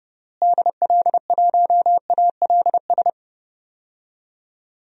Morse code, heard as a keyed tone, about 30 wpm, 700 Hz. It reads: DL1ALS
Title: DXCC-100 Call Signs - 6 Characters - Repeated 3x - 30wpm